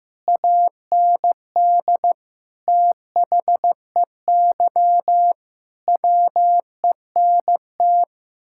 Morse code, heard as a keyed tone, about 15 wpm, 700 Hz. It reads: AND THEY WENT
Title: Sentences from Top 1600 Words 15wpm